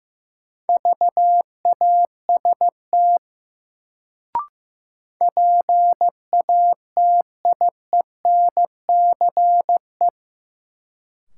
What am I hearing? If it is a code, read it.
Morse code, heard as a keyed tone, about 16 wpm, 700 Hz. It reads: VAST PATIENCE
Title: Sentences from Top 1300 Words - New Word Drill Alt - 15wpm